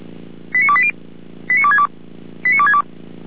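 Five-tone selective call (ZVEI), heard as quick runs of five tones, tones of about 70 ms. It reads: 89280 89262 89262
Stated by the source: ZVEI five-tone selective call, received by SDR.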